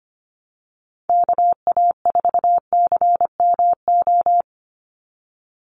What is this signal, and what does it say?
Morse code, heard as a keyed tone, about 25 wpm, 700 Hz. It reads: KA4CMO
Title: USA Call Signs 25wpm